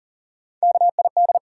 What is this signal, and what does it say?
Morse code, heard as a keyed tone, about 40 wpm, 700 Hz. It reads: KID
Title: Sentences from Top 1600 Words - New Word Drill Alt - 40wpm